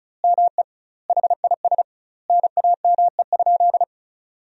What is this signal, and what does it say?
Morse code, heard as a keyed tone, about 36 wpm, 700 Hz. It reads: ME HIS NAME?
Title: 2x Word-Spacing - Sentences from Top 400 Words - 35wpm